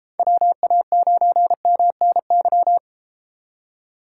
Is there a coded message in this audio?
Yes: Morse code from 0.2 to 2.8 s, heard as a keyed tone, about 30 wpm, 700 Hz. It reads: WA9MNY